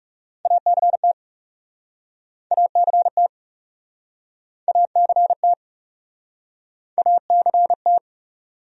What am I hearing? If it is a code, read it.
Morse code, heard as a keyed tone, about 44 wpm, 700 Hz. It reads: ACT ACT ACT ACT